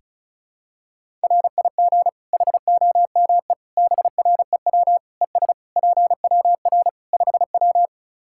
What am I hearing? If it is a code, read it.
Morse code, heard as a keyed tone, about 34 wpm, 700 Hz. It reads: RIG HOME BREW ES PWR 5W
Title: Practice QSO - Exchange 2 - Info - 35wpm